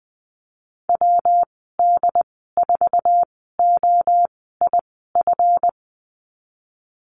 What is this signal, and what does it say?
Morse code, heard as a keyed tone, about 20 wpm, spaced as for 15 wpm, 700 Hz. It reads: WD4OIF
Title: 154 Lesson 25g - Letter p - Call Signs 20wpm